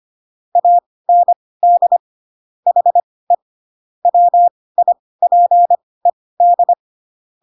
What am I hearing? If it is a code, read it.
Morse code, heard as a keyed tone, about 25 wpm, spaced as for 18 wpm, 700 Hz. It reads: AND HE WIPED
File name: Pinocchio ch3 12 25 ik1hge